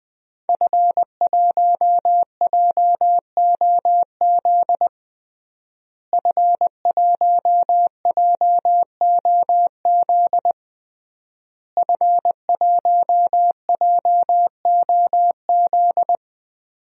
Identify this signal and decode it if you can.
Morse code, heard as a keyed tone, about 20 wpm, 700 Hz. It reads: F1JOZ F1JOZ F1JOZ
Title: DXCC-100 Call Signs - 5 Characters - Repeated 3x - 20wpm